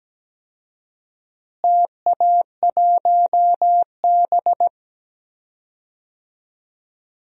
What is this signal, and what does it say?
Morse code, heard as a keyed tone, about 17 wpm, 700 Hz. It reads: TA1B
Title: DXCC-100 Call Signs - 4 Characters - Repeated 3x - 17wpm